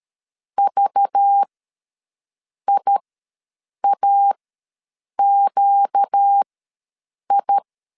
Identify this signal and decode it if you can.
Morse code, heard as a keyed tone, about 13 wpm, spaced as for 7 wpm, 800 Hz. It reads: VIAQI
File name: LEZIONI / 5 LBVC / 5 LBVC ws 12wpm 3159